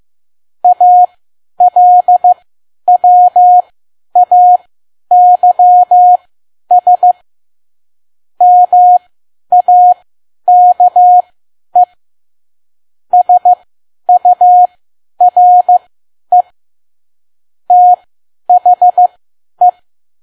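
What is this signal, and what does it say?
Morse code, heard as a keyed tone, about 15 wpm, spaced as for 10 wpm, 720 Hz. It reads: ALWAYS MAKE SURE THE
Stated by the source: Morse code practice at 10 WPM.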